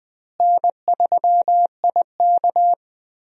Morse code, heard as a keyed tone, about 20 wpm, 700 Hz. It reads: N3IK
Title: USA Call Signs - 4 Characters - Repeated 3x - 20wpm